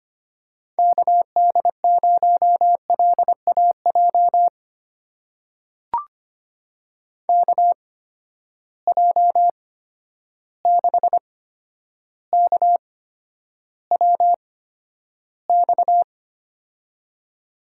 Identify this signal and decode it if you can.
Morse code, heard as a keyed tone, about 25 wpm, spaced as for 11 wpm, 700 Hz. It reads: KD0LAJ KJ6KWX
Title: USA Call Signs - 6 Characters - 8x Character Spacing - 25wpm